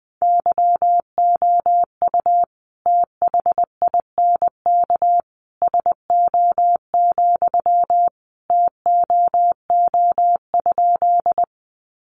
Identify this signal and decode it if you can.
Morse code, heard as a keyed tone, about 20 wpm, 700 Hz. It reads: YOU THINK SO, TOO?
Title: Sentences from Top 200 Words 20wpm